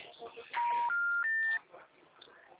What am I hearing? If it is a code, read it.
Special information tone, with speech.